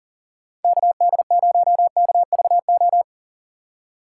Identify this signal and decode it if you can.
Morse code, heard as a keyed tone, about 40 wpm, 700 Hz. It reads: KD0KVO
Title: USA Call Signs - 6 Characters - 2x Character Spacing - 40wpm